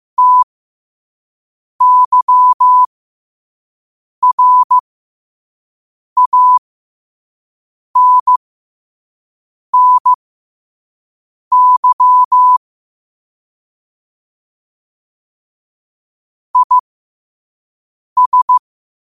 Morse code, heard as a keyed tone, about 15 wpm, spaced as for 5 wpm, 1000 Hz. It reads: TYRANNY IS
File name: qotdfiles / 5wpm